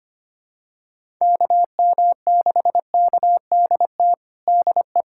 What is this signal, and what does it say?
Morse code, heard as a keyed tone, about 25 wpm, 700 Hz. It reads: KM6KDT DE
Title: USA Callsign DE Callsign - 6 Characters - 25wpm